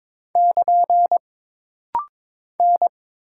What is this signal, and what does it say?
Morse code, heard as a keyed tone, about 22 wpm, 700 Hz. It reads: <KN> N
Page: This rapid-fire timing is geared to push your ICR (Instant Character Recognition) skills.